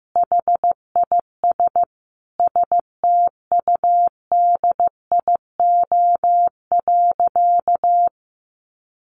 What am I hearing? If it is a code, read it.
Morse code, heard as a keyed tone, about 14 wpm, 700 Hz. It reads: HIS STUDIO.
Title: Sentences from Top 1200 Words 15wpm